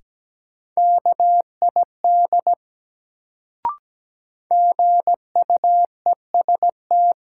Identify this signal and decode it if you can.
Morse code, heard as a keyed tone, about 17 wpm, 700 Hz. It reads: KID GUEST
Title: Sentences from Top 1600 Words - New Word Drill Alt - 17wpm